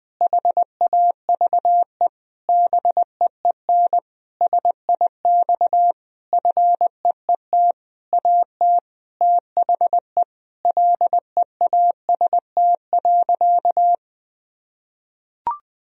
Morse code, heard as a keyed tone, about 21 wpm, 700 Hz. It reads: HAVE BEEN SIX FEET AT THE LEAST.
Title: Sentences from Top 1000 Words - Review Part 2 - 20wpm